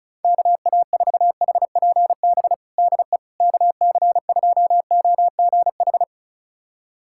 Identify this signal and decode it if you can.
Morse code, heard as a keyed tone, about 36 wpm, 700 Hz. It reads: KA4HPB DE KC2OGH
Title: USA Callsign DE Callsign 35wpm